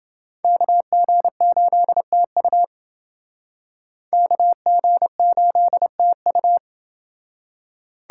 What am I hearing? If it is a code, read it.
Morse code, heard as a keyed tone, about 29 wpm, 700 Hz. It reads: KG8TU KG8TU